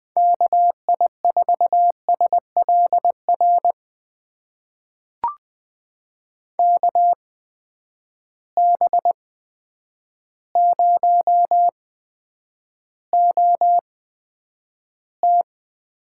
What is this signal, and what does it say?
Morse code, heard as a keyed tone, about 20 wpm, spaced as for 9 wpm, 700 Hz. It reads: KI4SLR KB0OT
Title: USA Call Signs - 6 Characters - 8x Character Spacing - 20wpm